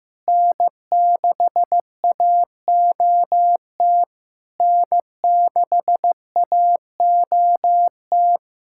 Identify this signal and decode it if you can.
Morse code, heard as a keyed tone, about 15 wpm, 700 Hz. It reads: N6AOT N6AOT
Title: SOTA QSO Practice 15wpm